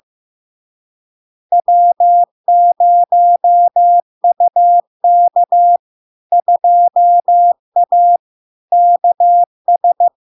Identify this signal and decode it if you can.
Morse code, heard as a keyed tone, about 15 wpm, 700 Hz. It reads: W0UK 2A KS
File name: LARK-FD-2022-Log-in-CW-15wpm-700-Hz